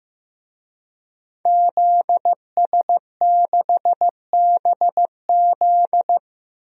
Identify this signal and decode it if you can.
Morse code, heard as a keyed tone, about 15 wpm, 700 Hz. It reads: ZS6BZ
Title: DXCC-100 Call Signs - 5 Characters - Repeated 2x - 15wpm